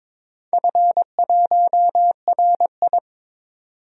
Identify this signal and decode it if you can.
Morse code, heard as a keyed tone, about 21 wpm, 700 Hz. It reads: F1RI
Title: DXCC-100 Call Signs - 4 Characters - 22wpm